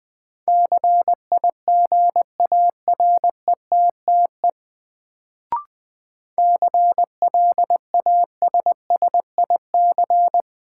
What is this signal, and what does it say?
Morse code, heard as a keyed tone, about 20 wpm, 700 Hz. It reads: CIGARETTE CLASSIC
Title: Sentences from Top 1000 Words - New Word Drill - 20wpm